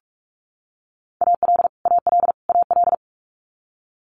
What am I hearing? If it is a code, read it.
Morse code, heard as a keyed tone, about 46 wpm, 700 Hz. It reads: AL AL AL
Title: Multi-letter Phonemes Repeated 3x 45wpm